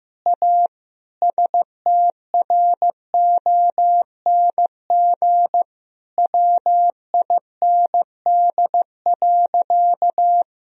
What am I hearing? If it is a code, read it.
Morse code, heard as a keyed tone, about 15 wpm, 700 Hz. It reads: A STRONG WIND.
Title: Sentences from Top 1000 Words - Review Part 4 - 15wpm